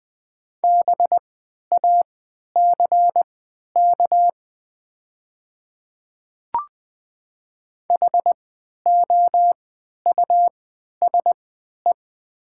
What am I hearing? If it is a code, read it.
Morse code, heard as a keyed tone, about 21 wpm, spaced as for 11 wpm, 700 Hz. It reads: BACK HOUSE
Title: 3x Character Spacing - Top 200 Words - 20wpm